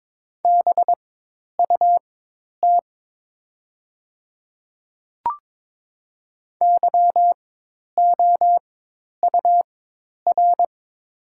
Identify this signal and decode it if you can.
Morse code, heard as a keyed tone, about 23 wpm, spaced as for 10 wpm, 700 Hz. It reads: BUT YOUR
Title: 4x Character Spacing - Top 500 Words - 22wpm